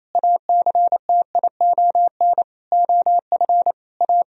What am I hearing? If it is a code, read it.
Morse code, heard as a keyed tone, about 28 wpm, 700 Hz. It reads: ACTION OF A
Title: Sentences from Top 1000 Words - Review Part 4 - 28wpm